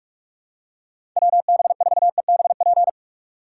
Morse code, heard as a keyed tone, about 45 wpm, 690 Hz. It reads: WB4EBP
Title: USA Call Signs - 4x Character Spacing - 45wpm